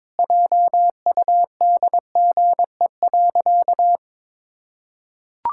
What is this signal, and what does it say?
Morse code, heard as a keyed tone, about 21 wpm, 700 Hz. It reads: JUDGE.
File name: practice / 22wpm / Sentences with 3-Word Phrases - 22wpm